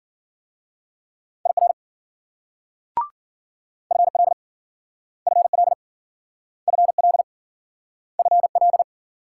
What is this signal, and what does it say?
Morse code, heard as a keyed tone, about 50 wpm, 700 Hz. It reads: IR FL FL FL FL
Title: Speed-Racing Multi-letter Phonemes - 40wpm